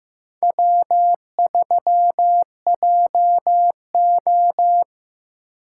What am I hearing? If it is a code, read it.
Morse code, heard as a keyed tone, about 15 wpm, 700 Hz. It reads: W3JO
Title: USA Call Signs - 4 Characters - 8x Character Spacing - 15wpm